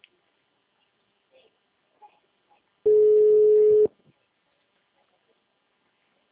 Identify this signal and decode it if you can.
Ringback tone (Europe, 425 Hz).